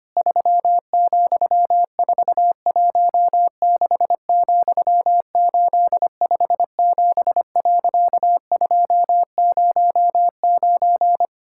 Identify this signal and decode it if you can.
Morse code, heard as a keyed tone, about 25 wpm, 700 Hz. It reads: 3,416,857.209
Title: Pangram Challenge v2 - 25wpm